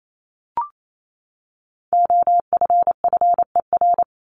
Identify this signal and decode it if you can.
Morse code, heard as a keyed tone, about 28 wpm, 700 Hz. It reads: OFFER
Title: ED Suffix - Top 100 Words with Base Words 28wpm